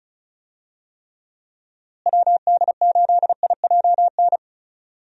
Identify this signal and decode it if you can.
Morse code, heard as a keyed tone, about 34 wpm, 700 Hz. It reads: WD8IJN
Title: USA Call Signs - 6 Characters - Repeated 2x - 35wpm